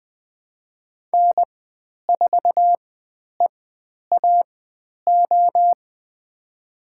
Morse code, heard as a keyed tone, about 20 wpm, spaced as for 10 wpm, 700 Hz. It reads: N4EAO
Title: 095 Lesson 16g - Letter c - Call Signs 20wpm